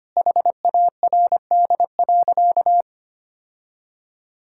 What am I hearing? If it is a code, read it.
Morse code, heard as a keyed tone, about 25 wpm, 700 Hz. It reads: HARD.
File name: practice / 25wpm / Sentences with 3-Word Phrases - 25wpm